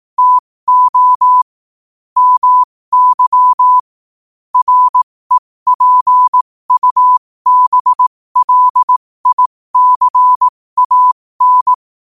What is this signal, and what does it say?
Morse code, heard as a keyed tone, about 18 wpm, spaced as for 15 wpm, 1000 Hz. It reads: TO MY REPUBLICAN
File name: qotdfiles / 15wpm